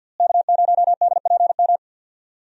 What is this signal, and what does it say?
Morse code, heard as a keyed tone, about 51 wpm, 700 Hz. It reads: K0DPM
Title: USA Call Signs - 5 Characters - Repeated 2x - 50wpm